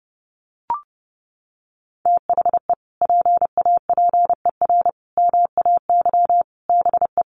Morse code, heard as a keyed tone, about 30 wpm, 700 Hz. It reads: THE PAPER MAY BE
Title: Sentences from Top 1400 Words 30wpm